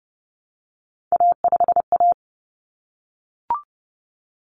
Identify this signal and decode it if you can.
Morse code, heard as a keyed tone, about 30 wpm, 700 Hz. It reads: A5A
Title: Common Cut Numbers - 3-Digit 30wpm